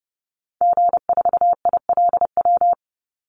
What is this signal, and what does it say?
Morse code, heard as a keyed tone, about 30 wpm, 700 Hz. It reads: G4ILW
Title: Call Signs Repeated 2x - 30wpm